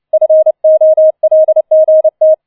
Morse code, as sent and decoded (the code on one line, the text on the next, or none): ..-. --- .-.. --. -
FOLGT